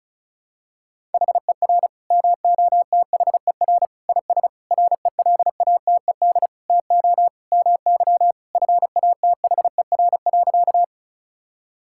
Morse code, heard as a keyed tone, about 36 wpm, 700 Hz. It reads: HER MOTHER IS RELATED TO MY FATHER.
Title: Sentences from Top 1000 Words - Review Part 5 - 35wpm